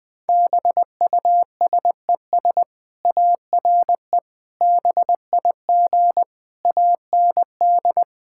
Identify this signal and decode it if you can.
Morse code, heard as a keyed tone, about 20 wpm, 700 Hz. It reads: BUSES ARE BIG AND
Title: Easy Sentences with Hints - 20wpm